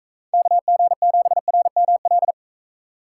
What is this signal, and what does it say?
Morse code, heard as a keyed tone, about 44 wpm, 700 Hz. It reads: KG7RML